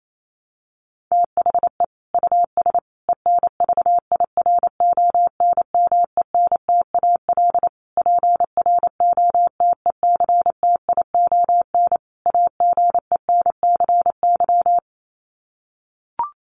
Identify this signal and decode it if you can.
Morse code, heard as a keyed tone, about 28 wpm, 700 Hz. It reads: THE US ENVIRONMENTAL PROTECTION AGENCY
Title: Sets of 5 Words Encore2 28wpm